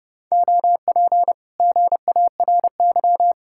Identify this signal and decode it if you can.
Morse code, heard as a keyed tone, about 40 wpm, 700 Hz. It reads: OP GARY